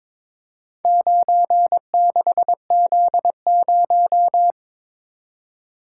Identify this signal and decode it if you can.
Morse code, heard as a keyed tone, about 22 wpm, 700 Hz. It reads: 96Z0
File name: practice / 22wpm / Four Letter-Number Combinations - 22wpm